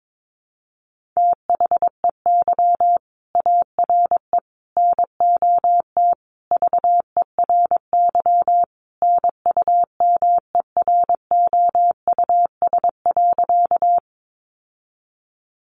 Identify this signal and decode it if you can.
Morse code, heard as a keyed tone, about 22 wpm, 700 Hz. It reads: THEY ARE NOT VERY NUMEROUS.
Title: Sentences from Top 1400 Words 22wpm